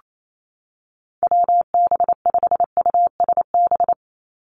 Morse code, heard as a keyed tone, about 28 wpm, 700 Hz. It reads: WB5USB
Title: NA SOTA Activator Callsigns 28wpm